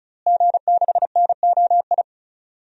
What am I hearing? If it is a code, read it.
Morse code, heard as a keyed tone, about 36 wpm, 700 Hz. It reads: G6NOI
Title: DXCC-100 Call Signs - 5 Characters - Repeated 3x - 35wpm